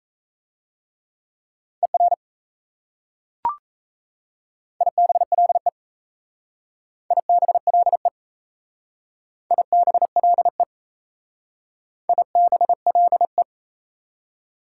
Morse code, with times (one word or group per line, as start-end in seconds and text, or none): 1.8-2.1 ER
4.8-5.7 IBLE
7.1-8.1 IBLE
9.5-10.6 IBLE
12.1-13.4 IBLE